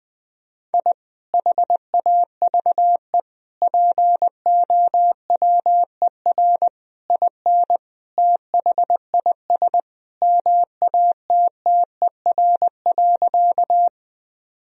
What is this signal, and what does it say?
Morse code, heard as a keyed tone, about 20 wpm, 700 Hz. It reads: I HAVE POWER IN THIS MATTER.
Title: Sentences from Top 1000 Words - Review Part 2 - 20wpm